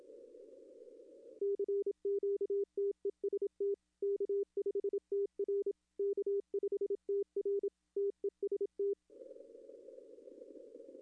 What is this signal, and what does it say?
Morse code, heard as a keyed tone, about 26 wpm, 410 Hz. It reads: CQ TEST K5TR K5TR TEST
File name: contests / arrl-160-k5tr-op-w5tn -0200